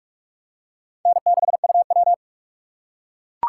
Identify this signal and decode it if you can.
Morse code, heard as a keyed tone, about 44 wpm, 700 Hz. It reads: N6UW
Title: USA Call Signs - 6x Character Spacing - 45wpm